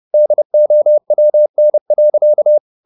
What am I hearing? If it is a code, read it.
Morse code, heard as a keyed tone, about 30 wpm, 600 Hz. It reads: DOWN.
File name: the old man and the sea / the old man and the sea 1 30wpm